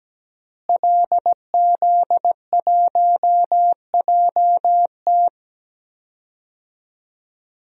Morse code, heard as a keyed tone, about 17 wpm, 700 Hz. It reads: LZ1JT
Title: DXCC-100 Call Signs - 5 Characters - Repeated 3x - 17wpm